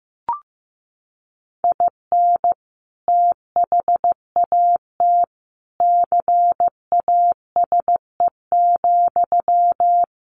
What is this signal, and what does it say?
Morse code, heard as a keyed tone, about 15 wpm, 700 Hz. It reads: IN THAT CASE,
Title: Sentences from Top 1400 Words 15wpm